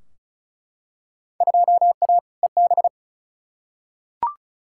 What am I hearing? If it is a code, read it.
Morse code, heard as a keyed tone, about 36 wpm, 700 Hz. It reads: 2A EB
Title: Field Day Exchanges - 35wpm